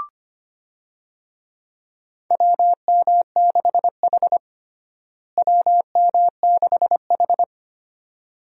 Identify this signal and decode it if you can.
Morse code, heard as a keyed tone, about 25 wpm, 700 Hz. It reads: WM6H WM6H
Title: USA Call Signs - 4 Characters - Repeated 2x - 25wpm